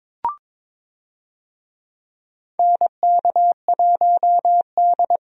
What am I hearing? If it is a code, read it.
Morse code, heard as a keyed tone, about 22 wpm, 700 Hz. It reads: NK1D
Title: USA Call Signs - 4 Characters - 22wpm